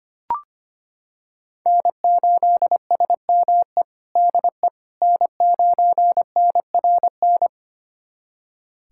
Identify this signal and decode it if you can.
Morse code, heard as a keyed tone, about 25 wpm, 700 Hz. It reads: N8SME DE N9NRN
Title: USA Callsign DE Callsign - 5 Characters - 25wpm